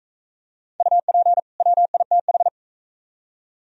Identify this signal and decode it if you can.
Morse code, heard as a keyed tone, about 43 wpm, 690 Hz. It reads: UP WITH